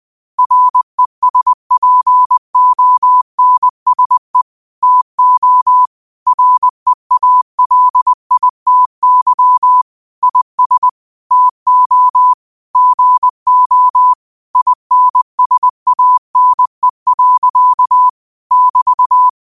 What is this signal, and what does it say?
Morse code, heard as a keyed tone, about 20 wpm, 1000 Hz. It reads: RESPONSE TO REALITY IS TO GO INSANE. <BT>
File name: qotdfiles / 20wpm